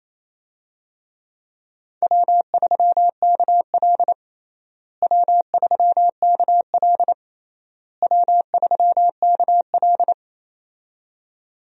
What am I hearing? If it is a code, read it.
Morse code, heard as a keyed tone, about 27 wpm, 700 Hz. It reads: W3KL W3KL W3KL
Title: USA Call Signs - 4 Characters - Repeated 3x - 28wpm